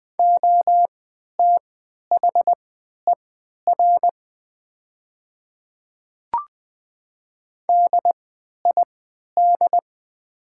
Morse code, heard as a keyed tone, about 21 wpm, spaced as for 11 wpm, 700 Hz. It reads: OTHER DID
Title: 3x Character Spacing - Top 200 Words - 20wpm